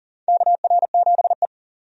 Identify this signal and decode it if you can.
Morse code, heard as a keyed tone, about 40 wpm, 700 Hz. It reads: KR7E
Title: USA Call Signs - 4 Characters - 40wpm